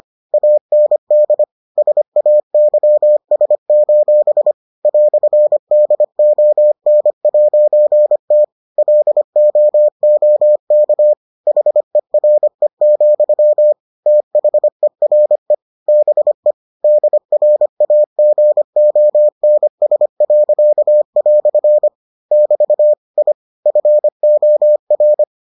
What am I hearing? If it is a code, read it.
Morse code, heard as a keyed tone, about 25 wpm, 600 Hz. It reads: AND SAYS: "DON'T LOOK HERE, THERE BE DRAGONS." <BT> I FOR